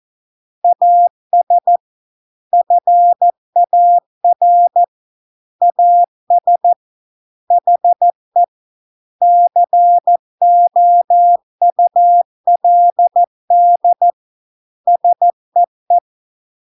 Morse code, heard as a keyed tone, about 14 wpm, 700 Hz. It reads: AS FAR AS HE COULD SEE